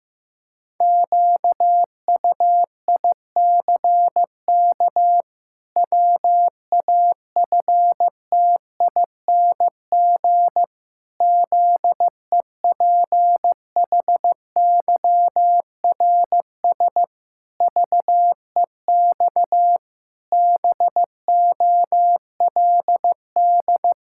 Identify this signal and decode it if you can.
Morse code, heard as a keyed tone, about 16 wpm, 700 Hz. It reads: QUICK WAFTING ZEPHYRS VEX BOLD
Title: Pangram Challenge v2 - 15wpm